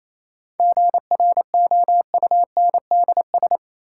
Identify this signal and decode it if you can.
Morse code, heard as a keyed tone, about 28 wpm, 700 Hz. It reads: GROUNDS
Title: Sentences from Top 1300 Words - New Word Drill - 28wpm